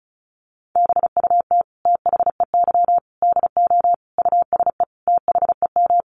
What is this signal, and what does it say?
Morse code, heard as a keyed tone, about 34 wpm, 700 Hz. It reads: BUT THEY DO USE THEM
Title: Sentences from Top 1300 Words 35wpm